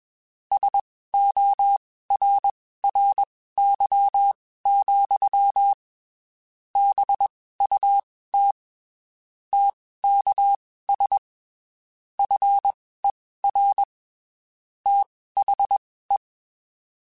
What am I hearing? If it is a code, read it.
Morse code, heard as a keyed tone, about 21 wpm, spaced as for 15 wpm, 800 Hz. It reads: SORRY, BUT TKS FER THE
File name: (spaced) / QSOs Spaced-mp3s / kc7sac @20 sp